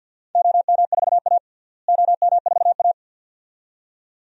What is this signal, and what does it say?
Morse code, heard as a keyed tone, about 52 wpm, 700 Hz. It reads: OM4A OM4A
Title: DXCC-100 Call Signs - 4 Characters - Repeated 2x - 50wpm